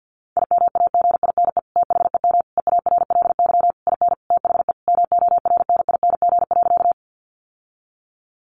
Morse code, heard as a keyed tone, about 50 wpm, 700 Hz. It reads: IMAGINE THEM EARLY IN THE MORNING.
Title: Sentences from Top 1400 Words 50wpm